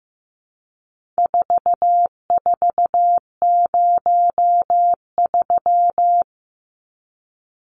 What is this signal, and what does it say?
Morse code, heard as a keyed tone, about 14 wpm, 700 Hz. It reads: 4403
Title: Numbers - 4-Digit 15wpm